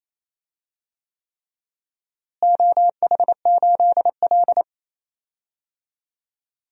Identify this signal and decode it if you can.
Morse code, heard as a keyed tone, about 28 wpm, 700 Hz. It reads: OH8L
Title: DXCC-100 Call Signs - 4 Characters - Repeated 3x - 28wpm